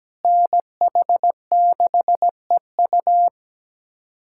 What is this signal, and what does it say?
Morse code, heard as a keyed tone, about 17 wpm, 700 Hz. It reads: NH6EU
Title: USA Callsign DE Callsign - 5 Characters - 17wpm